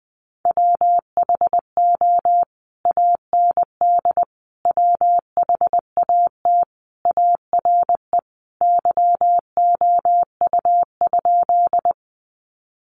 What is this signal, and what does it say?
Morse code, heard as a keyed tone, about 20 wpm, 700 Hz. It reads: WHO AND WHAT ARE YOU?
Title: Sentences from Top 100 Words 20wpm